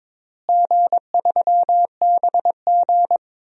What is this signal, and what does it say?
Morse code, heard as a keyed tone, about 22 wpm, 700 Hz. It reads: G3BG
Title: DXCC-100 Call Signs - 4 Characters - Repeated 2x - 22wpm